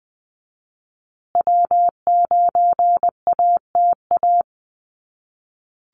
Morse code, heard as a keyed tone, about 20 wpm, 700 Hz. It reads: W9ATA
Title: USA Call Signs Repeated 3x 20wpm